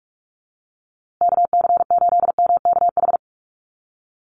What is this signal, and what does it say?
Morse code, heard as a keyed tone, about 44 wpm, 700 Hz. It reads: KC8MKH
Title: USA Call Signs Repeated 2x 45wpm